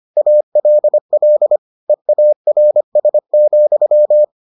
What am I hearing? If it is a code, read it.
Morse code, heard as a keyed tone, about 25 wpm, 600 Hz. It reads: ALL EARS,